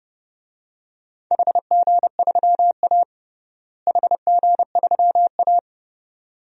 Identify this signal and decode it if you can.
Morse code, heard as a keyed tone, about 32 wpm, 700 Hz. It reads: HG3A HG3A
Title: DXCC-100 Call Signs - 4 Characters - Repeated 2x - 30wpm